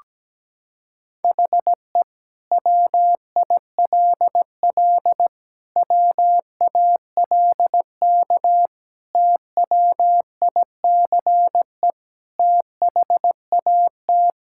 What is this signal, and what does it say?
Morse code, heard as a keyed tone, about 17 wpm, 700 Hz. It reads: HE WILL WALK TWICE THAT
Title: Sentences from Top 1600 Words 17wpm